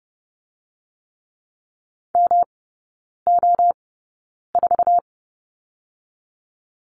Morse code, heard as a keyed tone, about 30 wpm, spaced as for 9 wpm, 700 Hz. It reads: MO4
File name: Lesson 19e - Letter w - Three Characters 30wpm